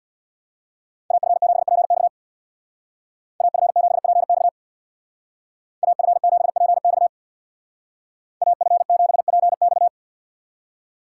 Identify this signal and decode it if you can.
Morse code, heard as a keyed tone, about 57 wpm, 700 Hz. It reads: AF7PX AF7PX AF7PX AF7PX
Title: Speed Racing - USA 2x2 Call Signs 50wpm